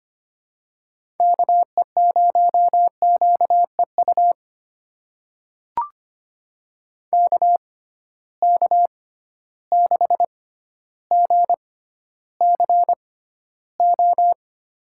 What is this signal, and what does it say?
Morse code, heard as a keyed tone, about 25 wpm, spaced as for 13 wpm, 700 Hz. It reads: KE0QEU KK6GCO
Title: USA Call Signs - 6 Characters - 6x Character Spacing - 25wpm